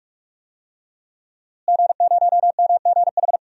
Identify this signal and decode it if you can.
Morse code, heard as a keyed tone, about 43 wpm, 700 Hz. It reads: G0MGH
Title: DXCC-100 Call Signs - 5 Characters - Repeated 2x - 45wpm